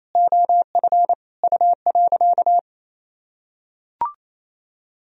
Morse code, heard as a keyed tone, about 28 wpm, 700 Hz. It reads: OF U.
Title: Sentences from Top 200 Words and Abbreviations 28wpm